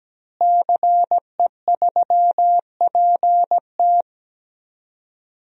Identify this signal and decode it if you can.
Morse code, heard as a keyed tone, about 17 wpm, 700 Hz. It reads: CE3PT
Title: DXCC-100 Call Signs - 5 Characters - Repeated 2x - 17wpm